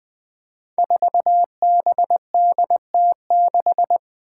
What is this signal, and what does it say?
Morse code, heard as a keyed tone, about 20 wpm, 700 Hz. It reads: 4BDT6
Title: Five Challenging Letter-Number Combinations 20wpm